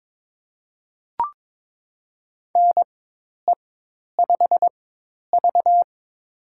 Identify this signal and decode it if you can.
Morse code, heard as a keyed tone, about 22 wpm, spaced as for 10 wpm, 700 Hz. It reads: NE5V
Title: USA Call Signs - 4x Character Spacing - 22wpm